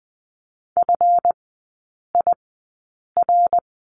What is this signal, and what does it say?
Morse code, heard as a keyed tone, about 20 wpm, spaced as for 8 wpm, 700 Hz. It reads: FIR
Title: 138 Lesson 23e - Letter f - Three Characters 20wpm